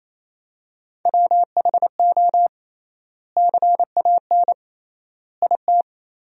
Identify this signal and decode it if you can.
Morse code, heard as a keyed tone, about 28 wpm, 700 Hz. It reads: WHO CAN IT
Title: 3x Word-Spacing - Sentences from Top 100 Words - 28wpm